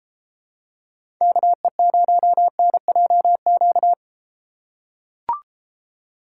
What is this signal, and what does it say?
Morse code, heard as a keyed tone, about 32 wpm, 700 Hz. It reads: KE0NJQ